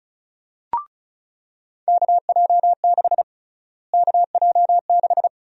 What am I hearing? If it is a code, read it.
Morse code, heard as a keyed tone, about 34 wpm, 700 Hz. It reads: KJ6 KJ6
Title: USA Call Sign Prefixes - Repeated 2x - 35wpm